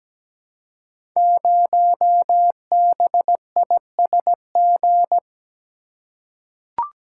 Morse code, heard as a keyed tone, about 17 wpm, 700 Hz. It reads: 0BISG
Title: Code Groups - 5 Letter:Numbers - 17wpm